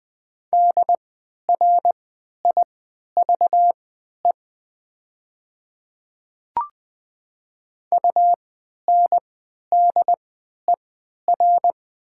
Morse code, heard as a keyed tone, about 20 wpm, spaced as for 11 wpm, 700 Hz. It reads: DRIVE UNDER
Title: 3x Character Spacing - Top 500 Words - 20wpm